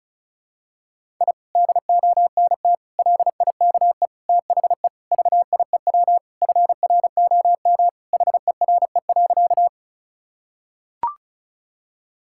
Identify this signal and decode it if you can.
Morse code, heard as a keyed tone, about 36 wpm, 700 Hz. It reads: I DONT LIKE THE VIEW FROM HERE.
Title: Sentences from Top 1000 Words - Review Part 6 - 35wpm